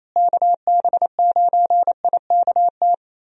Morse code, heard as a keyed tone, about 28 wpm, 700 Hz. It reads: KB9IKT
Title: USA Call Signs - 6 Characters - Repeated 3x - 28wpm